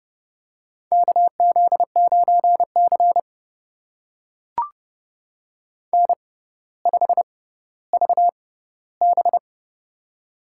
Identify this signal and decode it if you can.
Morse code, heard as a keyed tone, about 29 wpm, spaced as for 15 wpm, 700 Hz. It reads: KZ9C N5VB
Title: USA Call Signs - 4 Characters - 6x Character Spacing - 30wpm